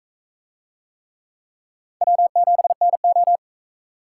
Morse code, heard as a keyed tone, about 42 wpm, 700 Hz. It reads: W7NO